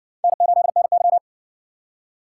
Morse code, heard as a keyed tone, about 54 wpm, 700 Hz. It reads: N8NY